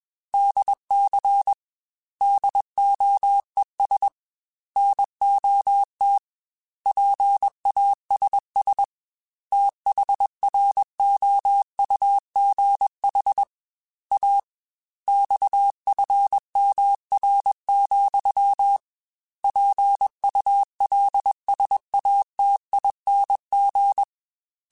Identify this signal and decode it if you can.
Morse code, heard as a keyed tone, about 21 wpm, 800 Hz. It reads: DC DOES NOT PASS THROUGH A XFMR, PULSATING
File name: Technical Subjects / Technical Subjects-mp3s / Sw Pwr Sup @20